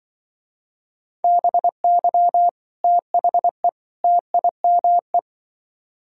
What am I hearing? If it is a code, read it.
Morse code, heard as a keyed tone, about 24 wpm, 700 Hz. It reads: BY THE TIME
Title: Speed-Racing - Sets of 3 Words 20wpm